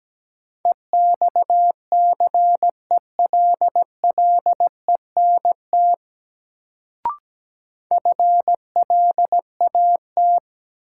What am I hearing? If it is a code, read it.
Morse code, heard as a keyed tone, about 17 wpm, 700 Hz. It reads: EXCELLENT FLAT
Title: Sentences from Top 1000 Words - New Word Drill - 17wpm